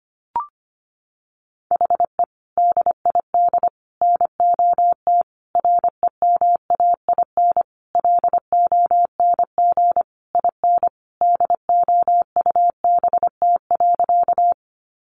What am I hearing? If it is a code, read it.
Morse code, heard as a keyed tone, about 25 wpm, 700 Hz. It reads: HE DID NOT REMAIN LONG IN DOUBT.
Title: Sentences from Top 1500 Words 25wpm